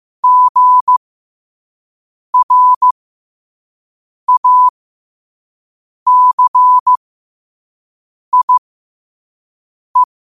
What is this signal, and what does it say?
Morse code, heard as a keyed tone, about 15 wpm, spaced as for 5 wpm, 1000 Hz. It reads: GRACIE
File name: qotdfiles / 5wpm